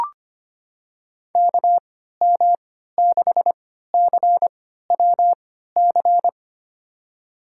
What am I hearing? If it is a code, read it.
Morse code, heard as a keyed tone, about 25 wpm, spaced as for 14 wpm, 700 Hz. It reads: KM6CWC
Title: USA Call Signs - 6 Characters - 3x Character Spacing - 25wpm